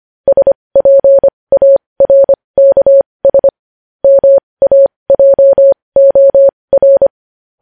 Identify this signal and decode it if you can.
Morse code, heard as a keyed tone, about 25 wpm, spaced as for 20 wpm, 550 Hz. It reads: SPARKS MAJOR
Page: News Headlines in Morse Code at 20 WPM